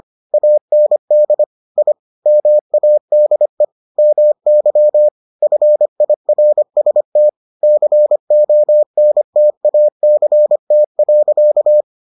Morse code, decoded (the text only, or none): AND I MADE MY FIRST CONTACT.